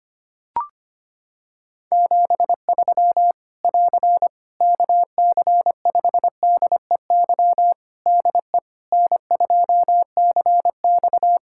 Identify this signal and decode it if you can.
Morse code, heard as a keyed tone, about 25 wpm, 700 Hz. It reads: 73 <AR> KC5DEY DE N2CX
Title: Practice QSO - Exchange 4 - Ending - 25wpm